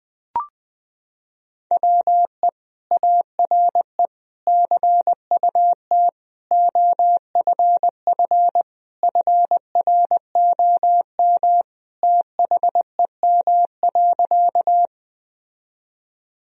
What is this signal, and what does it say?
Morse code, heard as a keyed tone, about 20 wpm, 700 Hz. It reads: WE ARE CUT OFF FROM THEM.
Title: Sentences from Top 1000 Words - Review Part 1 - 20wpm